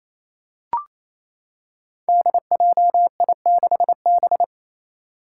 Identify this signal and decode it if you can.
Morse code, heard as a keyed tone, about 28 wpm, 700 Hz. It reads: DJI6B
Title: Five Challenging Letter-Number Combinations 28wpm